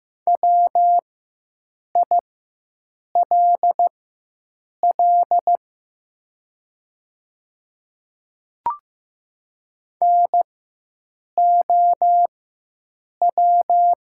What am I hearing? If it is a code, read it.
Morse code, heard as a keyed tone, about 15 wpm, spaced as for 7 wpm, 700 Hz. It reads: WILL NOW
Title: 4x Character Spacing - Top 100 Words - 15wpm